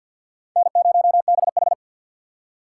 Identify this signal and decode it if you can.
Morse code, heard as a keyed tone, about 61 wpm, 700 Hz. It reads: N0BH
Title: Speed Racing - USA 1x2 Call Signs 50wpm